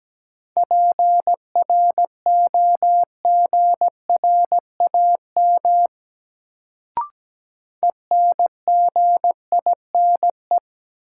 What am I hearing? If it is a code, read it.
Morse code, heard as a keyed tone, about 17 wpm, 700 Hz. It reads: PROGRAM ENGINE
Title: Sentences from Top 1500 Words - New Word Drill Alt - 17wpm